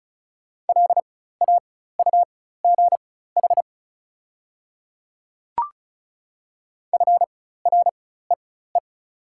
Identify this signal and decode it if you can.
Morse code, heard as a keyed tone, about 34 wpm, spaced as for 16 wpm, 700 Hz. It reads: LAUGH FREE
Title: 4x Character Spacing - Top 500 Words - 35wpm